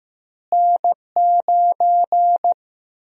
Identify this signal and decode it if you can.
Morse code, heard as a keyed tone, about 15 wpm, 700 Hz. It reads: N9
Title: USA Call Sign Prefixes - Repeated 2x - 15wpm